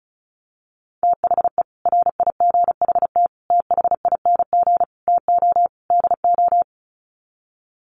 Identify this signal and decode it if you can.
Morse code, heard as a keyed tone, about 34 wpm, 700 Hz. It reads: THE RIGHT THING TO DO
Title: Sets of 5 Words 35wpm